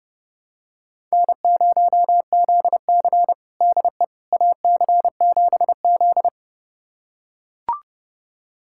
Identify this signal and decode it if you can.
Morse code, heard as a keyed tone, about 30 wpm, 700 Hz. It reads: N0ZC DE AC7Z
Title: USA Callsign DE Callsign - 4 Characters - 30wpm